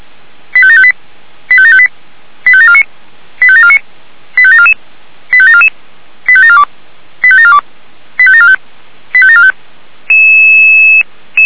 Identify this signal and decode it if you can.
Five-tone selective call (ZVEI), heard as quick runs of five tones, tones of about 70 ms. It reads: 85758 85758 85749 85749 85744 85744 85742 85742 85745 85745